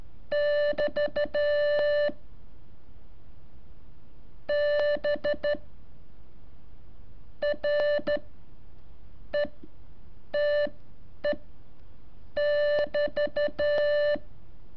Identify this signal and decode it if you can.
Morse code, heard as a keyed tone, about 13 wpm, spaced as for 7 wpm, 590 Hz. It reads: <BT> B RETE<BT>
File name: curso / ejercicio 18